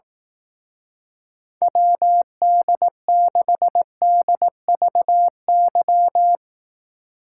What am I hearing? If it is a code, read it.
Morse code, heard as a keyed tone, about 18 wpm, 700 Hz. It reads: WD6DVY
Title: Speed Racing - USA 2x3 Call Signs 15wpm